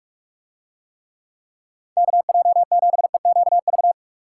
Morse code, heard as a keyed tone, about 42 wpm, 700 Hz. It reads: KJ7EQV